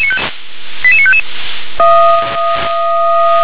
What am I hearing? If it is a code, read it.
Five-tone selective call (ZVEI), heard as quick runs of five tones, tones of about 70 ms. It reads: 88955